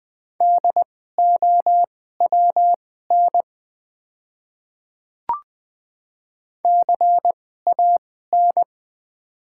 Morse code, heard as a keyed tone, about 20 wpm, spaced as for 14 wpm, 700 Hz. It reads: DOWN CAN
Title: 2x Character Spacing - Top 200 Words - 20wpm